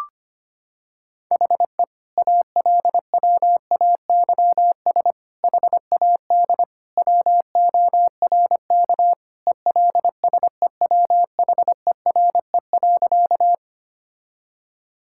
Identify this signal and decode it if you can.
Morse code, heard as a keyed tone, about 25 wpm, 700 Hz. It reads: HE ALWAYS HAD WORK ELSEWHERE.
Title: Sentences from Top 1600 Words 25wpm